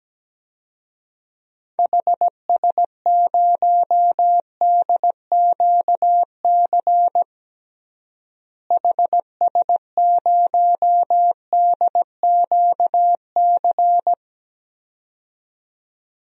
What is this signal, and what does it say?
Morse code, heard as a keyed tone, about 17 wpm, 700 Hz. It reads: HS0DQC HS0DQC
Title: DXCC-100 Call Signs - 6 Characters - Repeated 2x - 17wpm